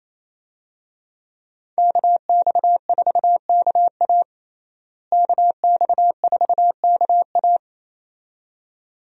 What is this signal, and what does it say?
Morse code, heard as a keyed tone, about 28 wpm, 700 Hz. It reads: KX4KA KX4KA
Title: USA Call Signs - 5 Characters - Repeated 2x - 28wpm